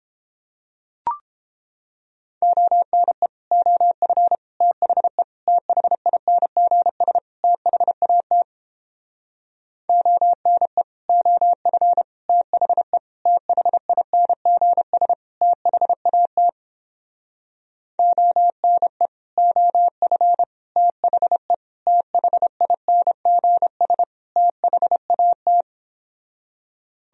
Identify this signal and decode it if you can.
Morse code, heard as a keyed tone, about 29 wpm, 700 Hz. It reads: ONE OF THE THINGS THAT ONE OF THE THINGS THAT ONE OF THE THINGS THAT